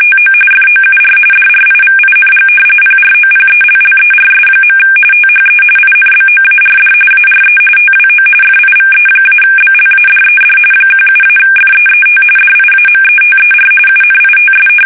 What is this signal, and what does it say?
RTTY, about 45.45 baud. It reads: TO FOUR DRIVING PROPELLERS. THE INBOARD ARE TWO FIVE-BLADED 17-FEET DIAMETER UNITS
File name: fsk850 / battleshipnj